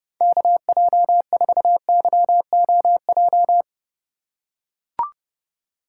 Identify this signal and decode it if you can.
Morse code, heard as a keyed tone, about 30 wpm, 700 Hz. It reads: KJ4YOJ
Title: USA Call Signs - 6 Characters - 2x Character Spacing - 30wpm